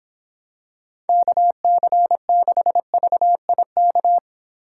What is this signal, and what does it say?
Morse code, heard as a keyed tone, about 25 wpm, 700 Hz. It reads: KC6VIK